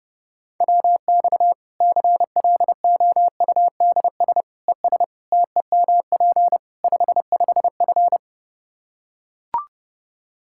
Morse code, heard as a keyed tone, about 30 wpm, 700 Hz. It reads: WX CLOUDS ES TEMP 55F
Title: K1USN SST QSO Practice 30wpm